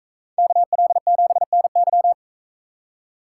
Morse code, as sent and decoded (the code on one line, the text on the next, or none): -.- .-.. --... -. -.--
KL7NY